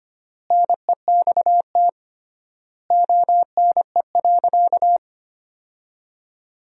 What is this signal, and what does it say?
Morse code, heard as a keyed tone, about 25 wpm, 700 Hz. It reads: NEXT ONE.
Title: 3x Word-Spacing - Sentences from Top 400 Words - 25wpm